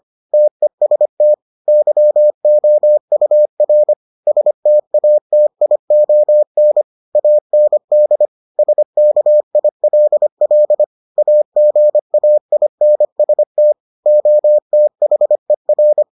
Morse code, heard as a keyed tone, about 25 wpm, 600 Hz. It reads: TEST YOUR STATION AND SKILL AGAINST OTHER